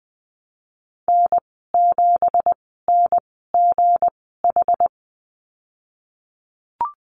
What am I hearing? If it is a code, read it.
Morse code, heard as a keyed tone, about 20 wpm, spaced as for 15 wpm, 700 Hz. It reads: N7NGH
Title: 193 Lesson 31g - Letter v - Call Signs 20wpm